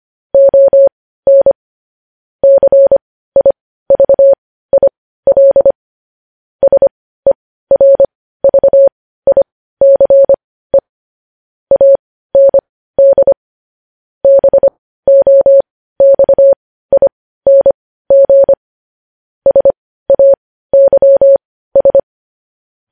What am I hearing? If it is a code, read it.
Morse code, heard as a keyed tone, about 25 wpm, spaced as for 15 wpm, 550 Hz. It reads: ON CIVIL SERVICE AND BOXING SAYS
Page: News Headlines in Morse Code at 15 WPM